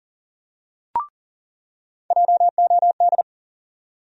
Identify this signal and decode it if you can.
Morse code, heard as a keyed tone, about 40 wpm, 700 Hz. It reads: JOD
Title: USA Call Sign Postfixes - 3 Letters 40wpm